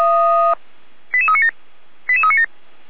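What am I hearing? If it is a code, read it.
Five-tone selective call (ZVEI), heard as quick runs of five tones, tones of about 70 ms. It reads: 88397 88397
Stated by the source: ZVEI five-tone selective call 88140, received by SDR, from the feuerwehr-orlamuende.org (Feuerwehr Orlamünde) server.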